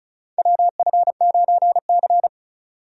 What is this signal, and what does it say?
Morse code, heard as a keyed tone, about 36 wpm, 690 Hz. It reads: WF9C
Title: USA Call Signs - 4 Characters - 35wpm